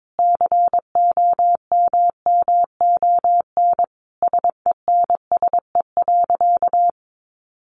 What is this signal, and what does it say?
Morse code, heard as a keyed tone, about 22 wpm, 700 Hz. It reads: COMMON SENSE.
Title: Sentences from Top 1300 Words 22wpm